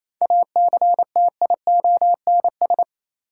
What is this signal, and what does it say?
Morse code, heard as a keyed tone, about 28 wpm, 700 Hz. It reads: ACTIONS
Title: Sentences from Top 1000 Words - Review Part 6 - New Word Drill - 28wpm